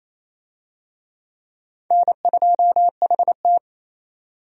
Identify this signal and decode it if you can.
Morse code, heard as a keyed tone, about 28 wpm, 700 Hz. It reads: N2HT
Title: USA Call Signs - 4 Characters - 28wpm